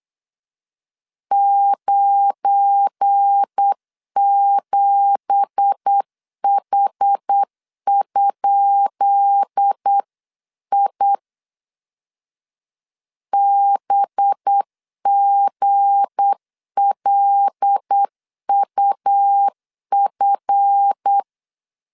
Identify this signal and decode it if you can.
Morse code, heard as a keyed tone, about 8 wpm, 800 Hz. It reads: 97H?I BGLUF
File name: LEZIONI / 10 SIGNS / 10B INT 8wpm 3534